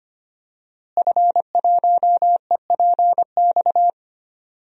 Morse code, heard as a keyed tone, about 25 wpm, 700 Hz. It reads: F1EPX
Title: DXCC-100 Call Signs - 5 Characters - Repeated 2x - 25wpm